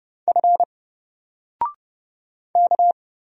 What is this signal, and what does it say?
Morse code, heard as a keyed tone, about 30 wpm, 700 Hz. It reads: F K
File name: Single-Letter-Number-30wpm